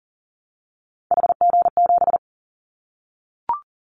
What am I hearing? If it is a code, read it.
Morse code, heard as a keyed tone, about 40 wpm, 700 Hz. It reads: HG7
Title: Callsign Prefixes for Top 50 DXCC - 40wpm